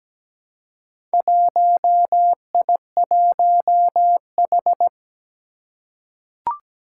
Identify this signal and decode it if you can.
Morse code, heard as a keyed tone, about 17 wpm, 700 Hz. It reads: 1I1H
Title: Four Challenging Letter-Number Combinations 17wpm